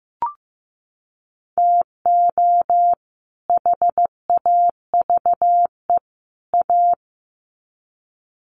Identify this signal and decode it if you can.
Morse code, heard as a keyed tone, about 15 wpm, 700 Hz. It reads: TO HAVE A
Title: Sets of 3 Words 15wpm